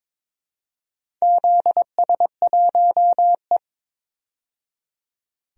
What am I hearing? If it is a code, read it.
Morse code, heard as a keyed tone, about 22 wpm, 700 Hz. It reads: ZS1E
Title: DXCC-100 Call Signs - 4 Characters - 22wpm